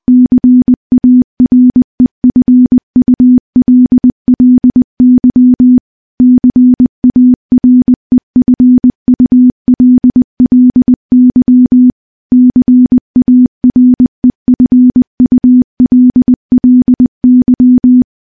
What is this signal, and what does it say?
Morse code, heard as a keyed tone, about 20 wpm, 250 Hz. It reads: CAREFULLY CAREFULLY CAREFULLY
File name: morse-training-20WPM